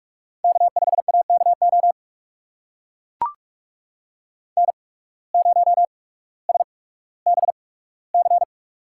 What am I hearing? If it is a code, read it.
Morse code, heard as a keyed tone, about 44 wpm, spaced as for 19 wpm, 700 Hz. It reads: K5AKO N0SBC
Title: USA Call Signs - 8x Character Spacing - 45wpm